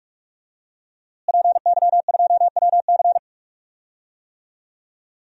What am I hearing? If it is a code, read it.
Morse code, heard as a keyed tone, about 46 wpm, 700 Hz. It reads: PY2WC
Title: DXCC-100 Call Signs - 5 Characters - Repeated 2x - 45wpm